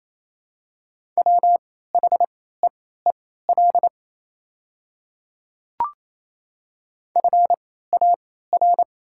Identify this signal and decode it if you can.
Morse code, heard as a keyed tone, about 27 wpm, spaced as for 16 wpm, 690 Hz. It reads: WHEEL FAR
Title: 3x Character Spacing - Top 500 Words - 28wpm